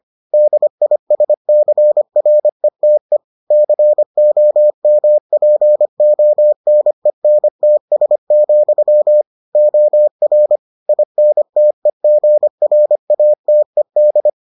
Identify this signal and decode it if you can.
Morse code, heard as a keyed tone, about 25 wpm, 600 Hz. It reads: DISCRETE COMPONENTS, OR INTEGRATED